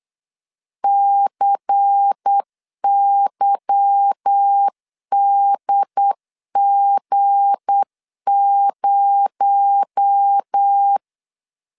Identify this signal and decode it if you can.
Morse code, heard as a keyed tone, about 8 wpm, 800 Hz. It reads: CYDG0
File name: LEZIONI / 8 YJ45 / 8 YJ4 8wpm 3246